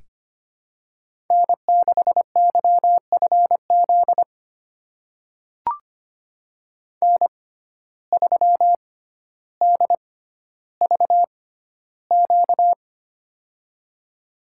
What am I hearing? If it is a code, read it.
Morse code, heard as a keyed tone, about 25 wpm, spaced as for 13 wpm, 700 Hz. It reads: N6YFZ N3DVQ
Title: USA Call Signs - 5 Characters - 6x Character Spacing - 25wpm